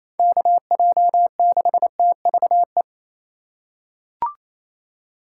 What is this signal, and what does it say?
Morse code, heard as a keyed tone, about 27 wpm, 700 Hz. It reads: KJ6TVE
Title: USA Call Signs - 2x Character Spacing - 28wpm